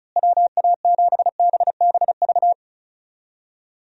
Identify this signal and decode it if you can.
Morse code, heard as a keyed tone, about 34 wpm, 700 Hz. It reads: WA7BBV
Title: USA Call Signs - 6 Characters - Repeated 3x - 35wpm